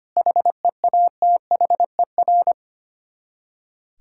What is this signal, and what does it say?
Morse code, heard as a keyed tone, about 25 wpm, 700 Hz. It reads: HEATHER
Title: K1USN SST QSO Practice 25wpm